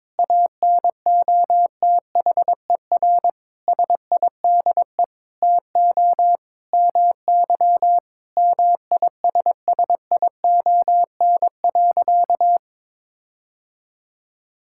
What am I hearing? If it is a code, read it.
Morse code, heard as a keyed tone, about 22 wpm, 700 Hz. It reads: ANOTHER SIDE TO MY MISSION.
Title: Sentences from Top 1600 Words 22wpm